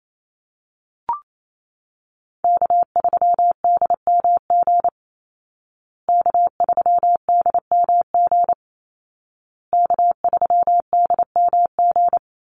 Morse code, heard as a keyed tone, about 28 wpm, 700 Hz. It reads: K3DMG K3DMG K3DMG
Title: Call Signs Repeated 2x - 28wpm